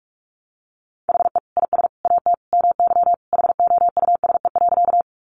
Morse code, heard as a keyed tone, about 44 wpm, 700 Hz. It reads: HE IS AT MY HOUSE.
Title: Sentences from Top 200 Words 45wpm